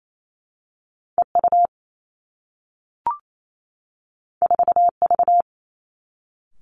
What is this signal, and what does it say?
Morse code, heard as a keyed tone, about 28 wpm, 700 Hz. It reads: E U 4 V
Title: Two Challenging Letter-Number Combinations 28wpm